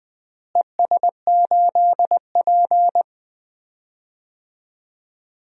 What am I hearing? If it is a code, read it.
Morse code, heard as a keyed tone, about 20 wpm, 700 Hz. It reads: ES8P
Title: DXCC-100 Call Signs - 4 Characters - Repeated 2x - 20wpm